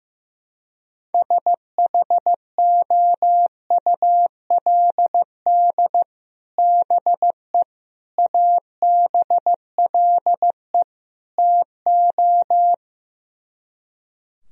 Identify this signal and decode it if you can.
Morse code, heard as a keyed tone, about 15 wpm, 700 Hz. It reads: SHOULD BE ABLE TO
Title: Speed-Racing - Sets of 4 Words 15wpm